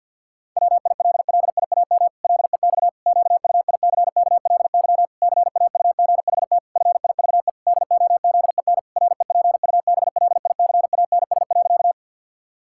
Morse code, heard as a keyed tone, about 50 wpm, 700 Hz. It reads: WILLIAM LEX QUICKLY CAUGHT FIVE DOZEN REPUBLICANS.
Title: Pangram Challenge v2 - 50wpm